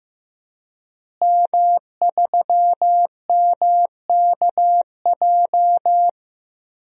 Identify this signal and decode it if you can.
Morse code, heard as a keyed tone, about 15 wpm, 700 Hz. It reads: M3MKJ
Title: DXCC-100 Call Signs - 5 Characters - Repeated 3x - 15wpm